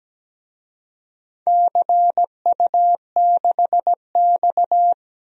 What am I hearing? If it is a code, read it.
Morse code, heard as a keyed tone, about 17 wpm, 700 Hz. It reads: CU6X
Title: DXCC-100 Call Signs - 4 Characters - 17wpm